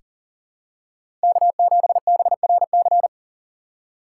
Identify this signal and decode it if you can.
Morse code, heard as a keyed tone, about 40 wpm, 700 Hz. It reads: K7BRC
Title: USA Call Signs - 3x Character Spacing - 40wpm